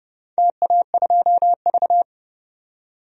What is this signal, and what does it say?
Morse code, heard as a keyed tone, about 32 wpm, 700 Hz. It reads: TA2V
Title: DXCC-100 Call Signs - 4 Characters - 30wpm